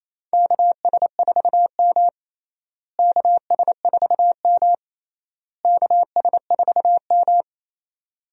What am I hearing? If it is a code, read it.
Morse code, heard as a keyed tone, about 27 wpm, 700 Hz. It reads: KS4M KS4M KS4M
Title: USA Call Signs - 4 Characters - Repeated 3x - 28wpm